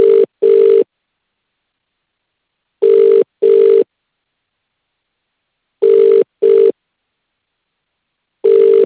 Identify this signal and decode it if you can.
Ringback tone (UK), 400 + 450 Hz.